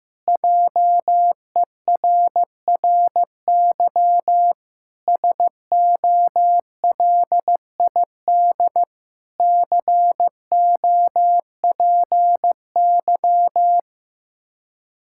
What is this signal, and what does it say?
Morse code, heard as a keyed tone, about 15 wpm, 700 Hz. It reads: JERRY SOLID COPY
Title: Practice QSO - Exchange 3 - More Info - 15wpm